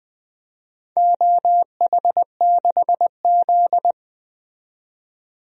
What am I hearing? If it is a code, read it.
Morse code, heard as a keyed tone, about 20 wpm, 700 Hz. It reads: OH6Z
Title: DXCC-100 Call Signs - 4 Characters - 20wpm